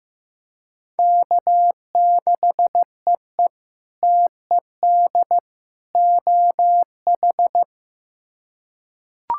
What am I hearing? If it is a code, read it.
Morse code, heard as a keyed tone, about 15 wpm, 700 Hz. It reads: K6EE TED OH
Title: K1USN SST QSO Practice - 15wpm